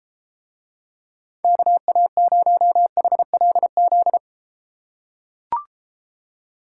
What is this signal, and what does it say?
Morse code, heard as a keyed tone, about 32 wpm, 700 Hz. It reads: KA0HLZ